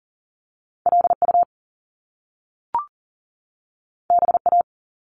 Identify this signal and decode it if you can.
Morse code, heard as a keyed tone, about 42 wpm, 700 Hz. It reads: LU BA
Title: Top Two Letter Combinations in Words 40wpm